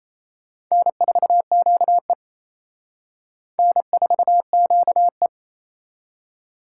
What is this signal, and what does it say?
Morse code, heard as a keyed tone, about 34 wpm, 700 Hz. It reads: N4QE N4QE
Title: Speed Racing - USA 1x2 Call Signs 28wpm